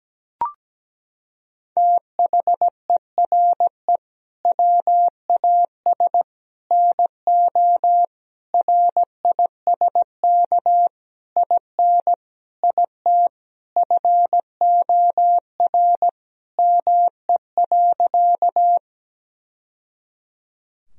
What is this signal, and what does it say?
Morse code, heard as a keyed tone, about 17 wpm, 700 Hz. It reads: THERE WAS NO RISK IN IT FOR ME.
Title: Sentences from Top 1000 Words - Review Part 4 - 17wpm